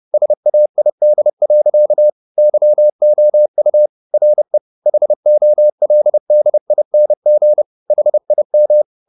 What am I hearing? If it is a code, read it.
Morse code, heard as a keyed tone, about 30 wpm, 600 Hz. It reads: SAID. YOU RE HOLDING HIM